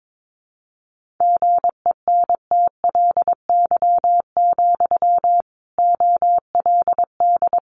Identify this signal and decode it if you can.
Morse code, heard as a keyed tone, about 22 wpm, 700 Hz. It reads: GENTLY, OLD
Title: Sentences from Top 1500 Words 22wpm